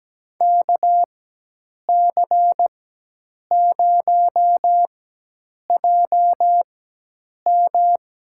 Morse code, heard as a keyed tone, about 17 wpm, spaced as for 8 wpm, 700 Hz. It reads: KC0JM
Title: USA Call Signs - 4x Character Spacing - 17wpm